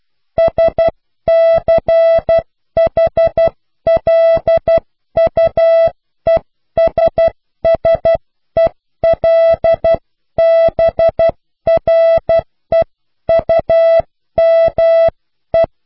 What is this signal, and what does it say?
Morse code, heard as a keyed tone, about 12 wpm, 660 Hz. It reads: SCHLUESSELBREUME
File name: cw / Pseudo woorden (15)